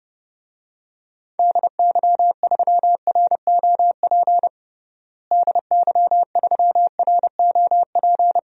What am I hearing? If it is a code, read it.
Morse code, heard as a keyed tone, about 32 wpm, 700 Hz. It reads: DY3ROP DY3ROP
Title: DXCC-100 Call Signs - 6 Characters - Repeated 2x - 30wpm